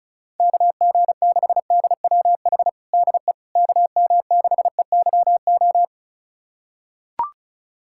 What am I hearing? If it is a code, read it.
Morse code, heard as a keyed tone, about 36 wpm, 700 Hz. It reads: KG6DWH DE KM6EYO
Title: USA Callsign DE Callsign - 6 Characters - 35wpm